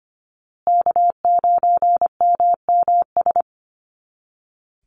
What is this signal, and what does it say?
Morse code, heard as a keyed tone, about 25 wpm, 700 Hz. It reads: K9MMS
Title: Call Signs Repeated 2x - 25wpm